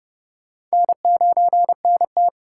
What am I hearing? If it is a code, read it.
Morse code, heard as a keyed tone, about 38 wpm, 700 Hz. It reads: N9NT
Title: Speed Racing - USA 1x2 Call Signs 30wpm